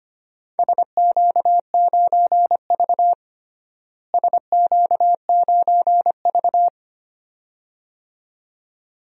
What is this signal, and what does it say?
Morse code, heard as a keyed tone, about 25 wpm, 700 Hz. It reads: SQ9V SQ9V
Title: DXCC-100 Call Signs - 4 Characters - Repeated 2x - 25wpm